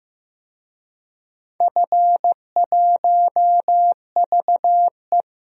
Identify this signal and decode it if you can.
Morse code, heard as a keyed tone, about 15 wpm, 700 Hz. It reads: F1VE
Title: DXCC-100 Call Signs - 4 Characters - 15wpm